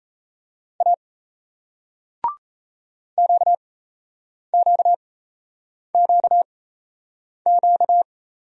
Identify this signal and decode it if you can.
Morse code, heard as a keyed tone, about 38 wpm, spaced as for 8 wpm, 700 Hz. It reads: A QQQQ